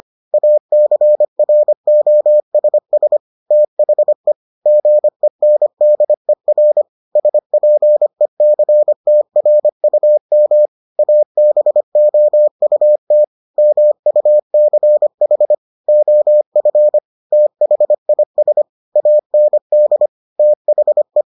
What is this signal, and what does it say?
Morse code, heard as a keyed tone, about 25 wpm, 600 Hz. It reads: ACROSS THE GENDER SPECTRUM ABOUT MUCH OF THIS AND THE